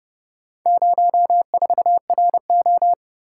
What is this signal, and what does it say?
Morse code, heard as a keyed tone, about 29 wpm, 700 Hz. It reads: 04RO